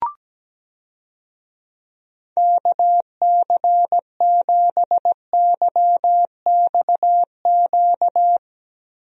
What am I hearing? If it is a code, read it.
Morse code, heard as a keyed tone, about 17 wpm, 700 Hz. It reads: KC7YXQ
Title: USA Call Signs - 6 Characters - Repeated 3x - 17wpm